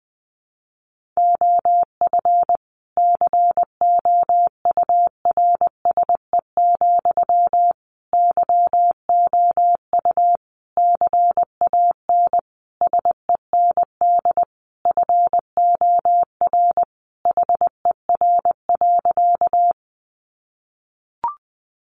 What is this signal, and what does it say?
Morse code, heard as a keyed tone, about 19 wpm, 700 Hz. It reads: OF COURSE, YOU CAN SEND FOR HER.
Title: Sentences from Top 1500 Words 20wpm